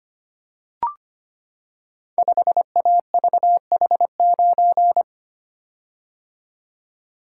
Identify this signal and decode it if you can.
Morse code, heard as a keyed tone, about 25 wpm, 690 Hz. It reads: 5AVH9
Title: Code Groups - 5 Letter:Numbers - 25wpm